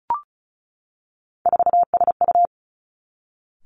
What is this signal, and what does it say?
Morse code, heard as a keyed tone, about 36 wpm, 700 Hz. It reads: 4SU
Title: Three Challenging Letter-Number Combinations 35wpm